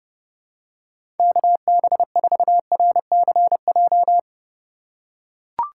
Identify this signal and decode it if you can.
Morse code, heard as a keyed tone, about 30 wpm, 700 Hz. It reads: KB4RCJ
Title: USA Call Signs - 6 Characters - 3x Character Spacing - 30wpm